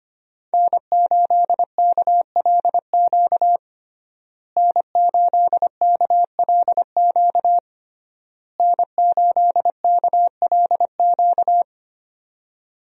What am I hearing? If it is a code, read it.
Morse code, heard as a keyed tone, about 25 wpm, 700 Hz. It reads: N8KLQ N8KLQ N8KLQ
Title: USA Call Signs - 5 Characters - Repeated 3x - 25wpm